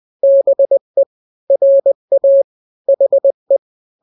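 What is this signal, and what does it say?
Morse code, heard as a keyed tone, about 20 wpm, 550 Hz. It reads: BE RA HE
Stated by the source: Morse drill of two-letter n-grams at 20/18 WPM (Farnsworth), from KK7RR.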